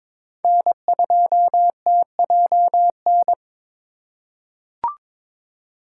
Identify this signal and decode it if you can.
Morse code, heard as a keyed tone, about 22 wpm, 700 Hz. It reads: N2TJN
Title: USA Call Signs - 5 Characters - 2x Character Spacing - 22wpm